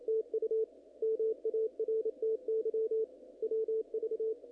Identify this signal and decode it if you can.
Morse code, heard as a keyed tone, about 27 wpm, 470 Hz, with radio noise. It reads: TU MARTY WV
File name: contests / naqp-cw-k5tr-op-k5pi- -1900